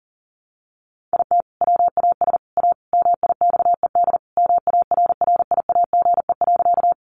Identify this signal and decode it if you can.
Morse code, heard as a keyed tone, about 40 wpm, 700 Hz. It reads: IT WAS A MIXED MARRIAGE.
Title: Sentences from Top 1500 Words 40wpm